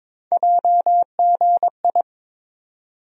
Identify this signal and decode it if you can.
Morse code, heard as a keyed tone, about 22 wpm, 700 Hz. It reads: JGI
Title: USA Call Sign Postfixes - 3 Letters 22wpm